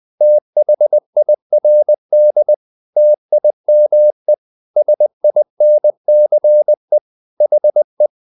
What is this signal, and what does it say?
Morse code, heard as a keyed tone, about 20 wpm, 600 Hz. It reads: THIRD TIME SINCE HE
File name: the old man and the sea part 2 20wpm